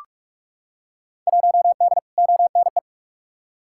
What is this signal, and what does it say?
Morse code, heard as a keyed tone, about 46 wpm, 700 Hz. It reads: 1D ONE
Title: Field Day Exchanges - 45wpm